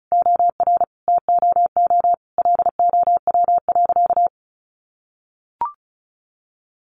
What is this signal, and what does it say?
Morse code, heard as a keyed tone, about 34 wpm, 700 Hz. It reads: OR TOO LOW.
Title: Sentences from Top 1200 Words 35wpm